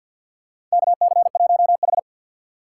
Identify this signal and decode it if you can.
Morse code, heard as a keyed tone, about 51 wpm, 700 Hz. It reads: KC1H
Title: USA Call Signs - 3x Character Spacing - 50wpm